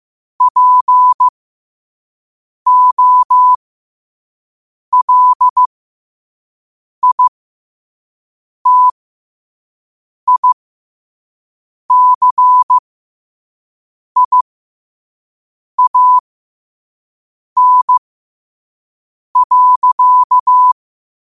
Morse code, heard as a keyed tone, about 15 wpm, spaced as for 5 wpm, 1000 Hz. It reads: POLITICIAN.
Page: qotd-at-5-wpm